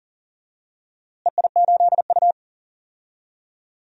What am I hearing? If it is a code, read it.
Morse code, heard as a keyed tone, about 40 wpm, 700 Hz. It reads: EI8U
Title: DXCC-100 Call Signs - 4 Characters - Repeated 2x - 40wpm